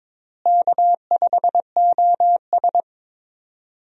Morse code, heard as a keyed tone, about 22 wpm, 700 Hz. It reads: K5OS
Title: USA Call Signs - 4 Characters - Repeated 3x - 22wpm